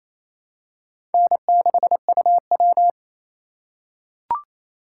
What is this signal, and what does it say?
Morse code, heard as a keyed tone, about 28 wpm, 700 Hz. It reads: N6UW
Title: USA Call Signs - 4 Characters - 4x Character Spacing - 28wpm